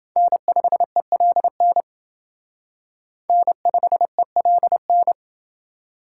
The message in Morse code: N5ELN N5ELN

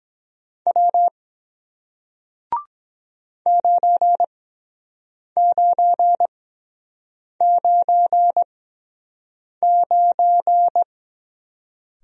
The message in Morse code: W 9999

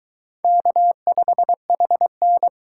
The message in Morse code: K5HN

